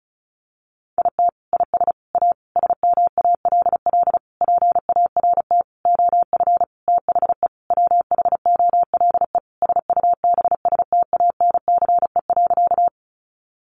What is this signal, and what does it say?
Morse code, heard as a keyed tone, about 34 wpm, 700 Hz. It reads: IT IS A SMALL PART OF THE WHOLE SUBSTANCE.
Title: Sentences from Top 1500 Words 35wpm